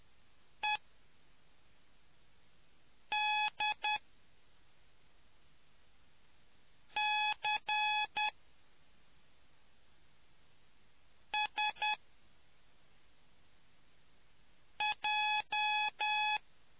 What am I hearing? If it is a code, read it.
Morse code, heard as a keyed tone, about 10 wpm, spaced as for 3 wpm, 850 Hz. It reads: EDCSJ